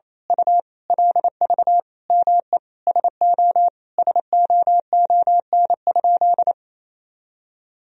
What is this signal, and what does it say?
Morse code, heard as a keyed tone, about 27 wpm, 700 Hz. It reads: U LV ME SO SOON?
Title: Sentences from Top 400 Words and Abbreviations 28wpm